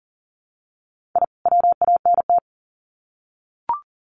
Morse code, heard as a keyed tone, about 40 wpm, 700 Hz. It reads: I WANT
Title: Sets of 2 Words 40wpm